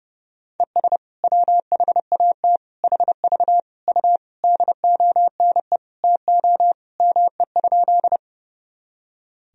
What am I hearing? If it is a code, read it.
Morse code, heard as a keyed tone, about 30 wpm, 700 Hz. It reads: ES WHAT HV U DONE TO ME?
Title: Sentences from Top 500 Words and Abbreviations 30wpm